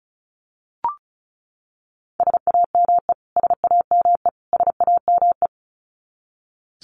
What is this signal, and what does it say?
Morse code, heard as a keyed tone, about 34 wpm, 700 Hz. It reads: SAME SAME SAME
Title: Top 300 Words Repeated 35wpm